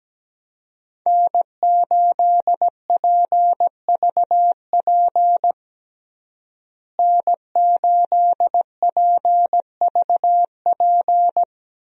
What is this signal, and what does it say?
Morse code, heard as a keyed tone, about 17 wpm, 700 Hz. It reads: N8PVP N8PVP
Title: USA Call Signs - 5 Characters - Repeated 3x - 17wpm